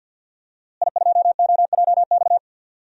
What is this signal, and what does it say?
Morse code, heard as a keyed tone, about 47 wpm, 700 Hz. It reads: I2OJX
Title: DXCC-100 Call Signs - 5 Characters - Repeated 3x - 50wpm